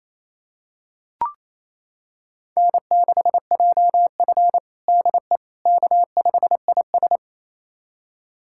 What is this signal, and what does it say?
Morse code, heard as a keyed tone, about 28 wpm, 700 Hz. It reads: N6JF DE K5IS
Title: USA Callsign DE Callsign - 4 Characters - 28wpm